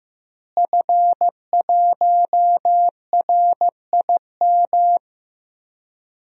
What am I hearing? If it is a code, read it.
Morse code, heard as a keyed tone, about 16 wpm, 700 Hz. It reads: F1RIM
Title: DXCC-100 Call Signs - 5 Characters - Repeated 2x - 15wpm